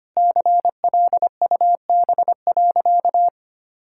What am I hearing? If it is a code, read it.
Morse code, heard as a keyed tone, about 25 wpm, 700 Hz. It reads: CLUB.
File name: practice / 25wpm / Sentences with 3-Word Phrases - 25wpm